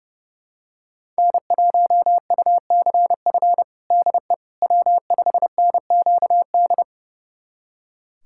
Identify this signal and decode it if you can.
Morse code, heard as a keyed tone, about 32 wpm, 700 Hz. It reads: N1UCF DE W5NQD
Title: USA Callsign DE Callsign - 5 Characters - 30wpm